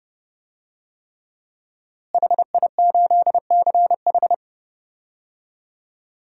Morse code, heard as a keyed tone, about 32 wpm, 700 Hz. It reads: HI8CH
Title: DXCC-100 Call Signs - 5 Characters - Repeated 3x - 30wpm